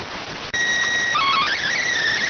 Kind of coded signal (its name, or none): SSTV (slow-scan television)